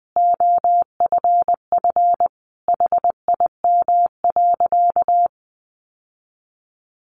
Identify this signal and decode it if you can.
Morse code, heard as a keyed tone, about 20 wpm, 700 Hz. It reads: OFF HIM.
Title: Sentences from Top 1300 Words 20wpm